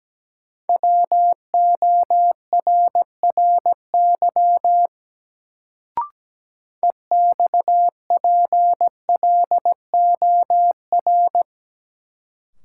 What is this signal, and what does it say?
Morse code, heard as a keyed tone, about 17 wpm, 700 Hz. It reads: WORRY EXPLOR
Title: Sentences from Top 1100 Words - New Word Drill Alt - 17wpm